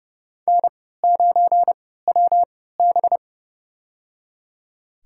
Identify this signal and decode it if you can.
Morse code, heard as a keyed tone, about 30 wpm, spaced as for 17 wpm, 700 Hz. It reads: N9WB
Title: USA Call Signs - 4 Characters - 3x Character Spacing - 30wpm